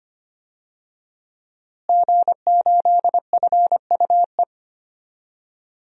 Morse code, heard as a keyed tone, about 25 wpm, 700 Hz. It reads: G8FUE
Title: DXCC-100 Call Signs - 5 Characters - Repeated 3x - 25wpm